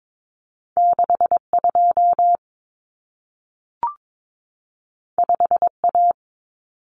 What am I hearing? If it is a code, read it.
Morse code, heard as a keyed tone, about 22 wpm, 700 Hz. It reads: 62 5A
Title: Common Cut Numbers - 2-Digit 22wpm